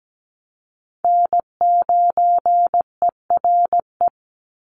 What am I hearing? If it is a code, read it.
Morse code, heard as a keyed tone, about 17 wpm, 700 Hz. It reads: N9ERE
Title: USA Call Signs 17wpm